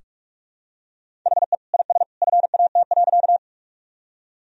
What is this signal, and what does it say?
Morse code, heard as a keyed tone, about 41 wpm, 700 Hz. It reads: HE IS FAT.
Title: Sentences from Top 1000 Words - Review Part 4 - 45wpm